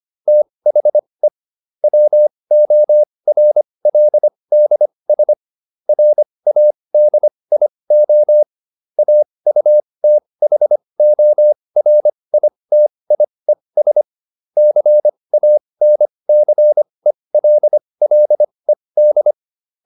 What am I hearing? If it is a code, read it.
Morse code, heard as a keyed tone, about 25 wpm, spaced as for 20 wpm, 600 Hz. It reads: THE WORLDS RADIO AUTHORITIES CANCELLED